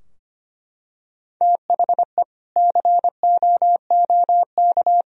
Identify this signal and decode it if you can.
Morse code, heard as a keyed tone, about 25 wpm, 700 Hz. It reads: THE COOK
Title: Sentences from Top 1000 Words - Review Part 4 - 25wpm